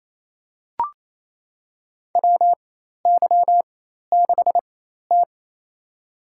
Morse code, heard as a keyed tone, about 28 wpm, spaced as for 13 wpm, 700 Hz. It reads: WY6T
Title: USA Call Signs - 4 Characters - 4x Character Spacing - 28wpm